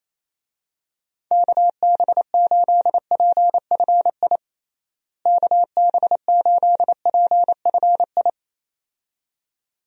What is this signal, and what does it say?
Morse code, heard as a keyed tone, about 27 wpm, 700 Hz. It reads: KB8PFI KB8PFI
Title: USA Call Signs - 6 Characters - Repeated 2x - 28wpm